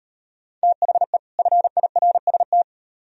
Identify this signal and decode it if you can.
Morse code, heard as a keyed tone, about 38 wpm, 700 Hz. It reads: THE FIRST